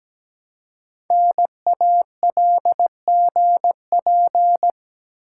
Morse code, heard as a keyed tone, about 17 wpm, 700 Hz. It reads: NALGP
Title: Code Groups - 5 Letters - 17wpm